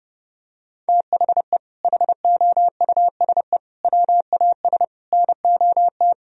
Morse code, heard as a keyed tone, about 32 wpm, 700 Hz. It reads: THE HOUSE WAS NOT
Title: Sentences from Top 1000 Words - Review Part 2 - 30wpm